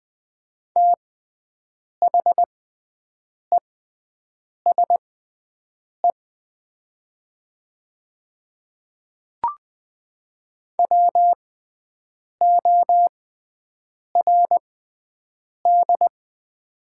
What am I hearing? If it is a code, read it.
Morse code, heard as a keyed tone, about 20 wpm, spaced as for 7 wpm, 700 Hz. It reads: THESE WORD
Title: 6x Character Spacing - Top 100 Words - 20wpm